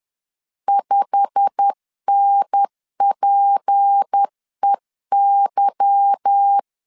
Morse code, heard as a keyed tone, about 10 wpm, 800 Hz. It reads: 5NPEY
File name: LEZIONI / 9 286 10wpm 2735